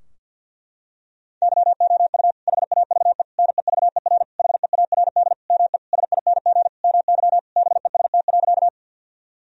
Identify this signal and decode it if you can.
Morse code, heard as a keyed tone, about 50 wpm, 700 Hz. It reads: YOU HAVE NEVER HEARD ME SING MY BEST.
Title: Sentences from Top 1000 Words - Review Part 2 - 50wpm